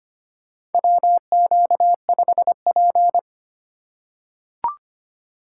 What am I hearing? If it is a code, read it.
Morse code, heard as a keyed tone, about 25 wpm, 700 Hz. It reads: WQ5P
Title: USA Call Signs - 4 Characters - 4x Character Spacing - 25wpm